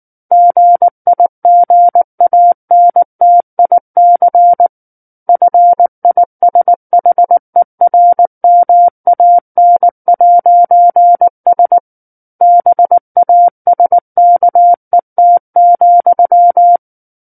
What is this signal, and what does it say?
Morse code, heard as a keyed tone, about 19 wpm, 700 Hz. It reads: GIGANTIC FISHERMAN'S BASKET,
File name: Book One / 10-Chapter 10 - 19 WPM